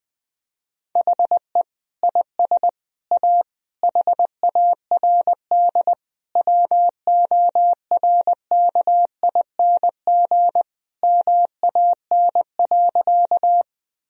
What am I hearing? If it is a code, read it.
Morse code, heard as a keyed tone, about 21 wpm, 700 Hz. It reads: HE IS A HARD WORKING MAN.
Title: Sentences from Top 1000 Words - Review Part 4 - 20wpm